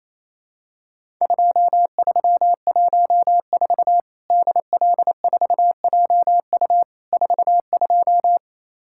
Morse code, heard as a keyed tone, about 28 wpm, 700 Hz. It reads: 2314 DL4JU 42
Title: Worked All Europe DX Contest - QTC Exchange - 28wpm